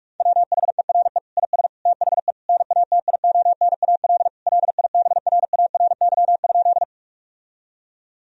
Morse code, heard as a keyed tone, about 41 wpm, 700 Hz. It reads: WHERE IS THE NATIONAL LIBRARY?
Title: Sentences from Top 1000 Words - Review Part 4 - 45wpm